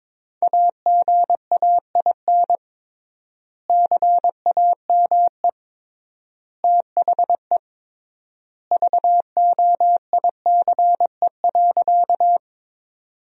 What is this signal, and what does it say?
Morse code, heard as a keyed tone, about 22 wpm, 700 Hz. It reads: AGAIN CAME THE VOICE.
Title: 3x Word-Spacing - Sentences from Top 500 Words - 22wpm